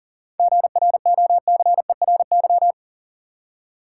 Morse code, heard as a keyed tone, about 40 wpm, 700 Hz. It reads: GROCERY
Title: Seven Letter Words - 40wpm